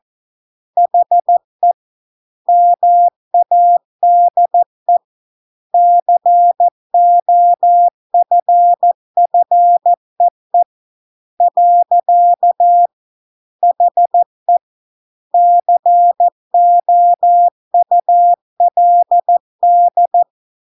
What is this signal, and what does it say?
Morse code, heard as a keyed tone, about 14 wpm, 700 Hz. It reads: HE MADE COFFEE . HE COULD